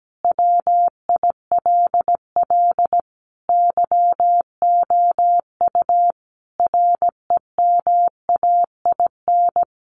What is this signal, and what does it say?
Morse code, heard as a keyed tone, about 17 wpm, 700 Hz. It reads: WILL YOU REMAIN
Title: Sentences from Top 1200 Words 17wpm